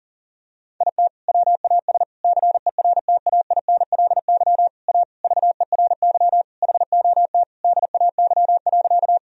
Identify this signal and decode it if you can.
Morse code, heard as a keyed tone, about 40 wpm, 700 Hz. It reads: IT WAS CERTAINLY A VERY HOT DAY.
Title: Sentences from Top 1000 Words - Review Part 3 - 40wpm